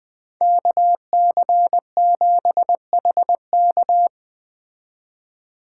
Morse code, heard as a keyed tone, about 20 wpm, 700 Hz. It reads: KC7HK